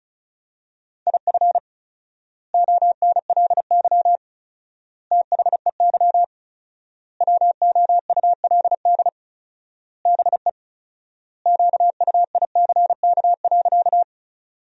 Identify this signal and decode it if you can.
Morse code, heard as a keyed tone, about 36 wpm, 700 Hz. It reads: IF ONLY THEY WOULD BE QUICK.
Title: 4x Word-Spacing - Sentences from Top 500 Words - 35wpm